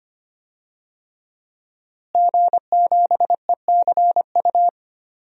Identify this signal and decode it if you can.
Morse code, heard as a keyed tone, about 25 wpm, 700 Hz. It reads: G7ECU
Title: DXCC-100 Call Signs - 5 Characters - Repeated 3x - 25wpm